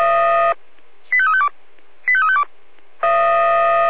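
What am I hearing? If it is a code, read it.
Five-tone selective call (ZVEI), heard as quick runs of five tones, tones of about 70 ms. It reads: 85352 85352
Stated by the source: ZVEI five-tone selective call 85359, received with an SDR.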